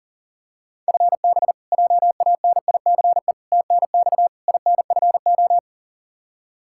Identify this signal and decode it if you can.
Morse code, heard as a keyed tone, about 40 wpm, 700 Hz. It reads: FB JANICE TNX INFO
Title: K1USN SST QSO Practice 40wpm